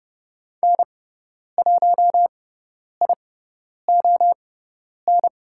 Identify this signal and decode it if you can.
Morse code, heard as a keyed tone, about 30 wpm, spaced as for 10 wpm, 700 Hz. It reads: N1ION
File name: Lesson 16g - Letter c - Call Signs 30wpm